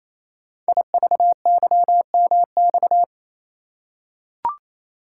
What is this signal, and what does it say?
Morse code, heard as a keyed tone, about 27 wpm, 700 Hz. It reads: IVYMX
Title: Code Groups - 5 Letters - 28wpm